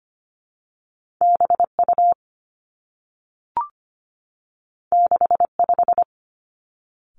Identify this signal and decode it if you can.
Morse code, heard as a keyed tone, about 25 wpm, 700 Hz. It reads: B U 6 5
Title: Two Challenging Letter-Number Combinations 25wpm